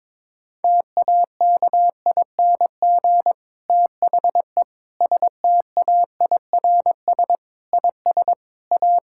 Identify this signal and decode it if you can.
Morse code, heard as a keyed tone, about 21 wpm, 700 Hz. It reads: TAKING THE STAIRS IS A
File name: Difficult Sentences with Hints - 22wpm